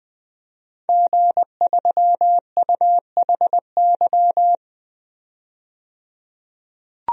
Morse code, heard as a keyed tone, about 20 wpm, 700 Hz. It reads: G3UHY
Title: DXCC-100 Call Signs - 5 Characters - 20wpm